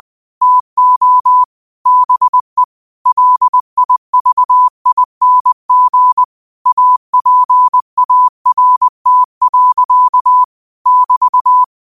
Morse code, heard as a keyed tone, about 20 wpm, 1000 Hz. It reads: TO BE LIVING APART. <BT>
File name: qotdfiles / 20wpm